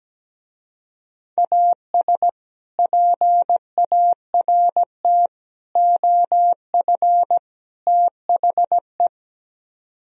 Morse code, heard as a keyed tone, about 17 wpm, 700 Hz. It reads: AS PART OF THE
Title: Speed-Racing - Sets of 4 Words 17wpm